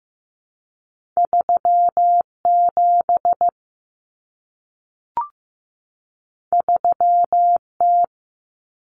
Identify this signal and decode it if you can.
Morse code, heard as a keyed tone, about 15 wpm, 700 Hz. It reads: 37 3T
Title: Common Cut Numbers - 2-Digit 15wpm